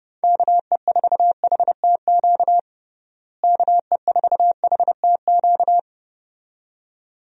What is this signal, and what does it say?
Morse code, heard as a keyed tone, about 30 wpm, 700 Hz. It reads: KE4HTQ KE4HTQ
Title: USA Call Signs - 6 Characters - Repeated 2x - 30wpm